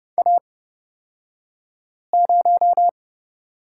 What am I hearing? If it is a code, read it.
Morse code, heard as a keyed tone, about 30 wpm, spaced as for 5 wpm, 700 Hz. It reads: A0
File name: Lesson 37d - Letter x - Two Characters 30wpm